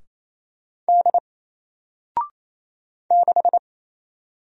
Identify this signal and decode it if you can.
Morse code, heard as a keyed tone, about 27 wpm, 700 Hz. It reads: D 6
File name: Single-Letter-Number-28wpm